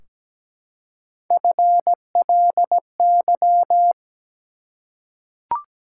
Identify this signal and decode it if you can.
Morse code, heard as a keyed tone, about 17 wpm, 700 Hz. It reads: FLY
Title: Three Letter Words - 17wpm